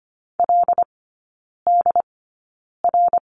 Morse code, heard as a keyed tone, about 25 wpm, spaced as for 9 wpm, 700 Hz. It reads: LDR
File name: Lesson 17e - Letter u - Three Characters 25wpm